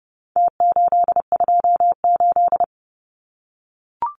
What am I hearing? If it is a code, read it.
Morse code, heard as a keyed tone, about 30 wpm, 700 Hz. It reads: T828
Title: Numbers - Contest Style - 4-Digit 30wpm